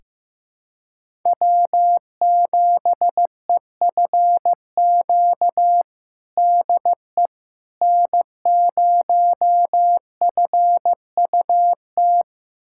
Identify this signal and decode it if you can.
Morse code, heard as a keyed tone, about 15 wpm, 700 Hz. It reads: W7EFQ DE N0FUT
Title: USA Callsign DE Callsign - 5 Characters - 15wpm